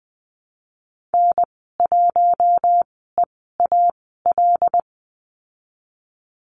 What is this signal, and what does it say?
Morse code, heard as a keyed tone, about 20 wpm, spaced as for 14 wpm, 700 Hz. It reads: N1EAL
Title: 076 Lesson 13g - Letter L - Call Signs 20wpm